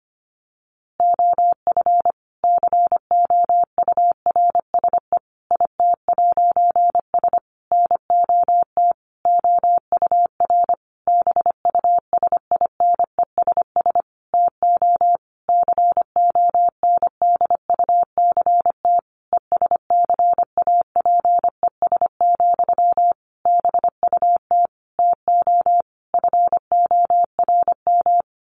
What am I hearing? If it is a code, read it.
Morse code, heard as a keyed tone, about 25 wpm, 700 Hz. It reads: OF COURSE IT'S NOT OUR BUSINESS TO CONDUCT ESCAPES, BUT TO FORM
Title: A Sound Of Thunder 25wpm